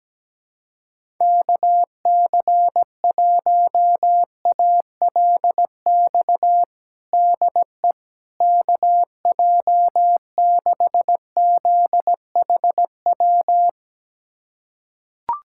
Morse code, heard as a keyed tone, about 17 wpm, 700 Hz. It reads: KC1ALX DE KJ6ZHW
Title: USA Callsign DE Callsign - 6 Characters - 17wpm